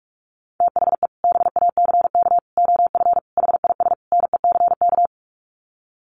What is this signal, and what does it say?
Morse code, heard as a keyed tone, about 44 wpm, 700 Hz. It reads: THE BACK OF HIS NECK
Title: Sets of 5 Words Encore 45wpm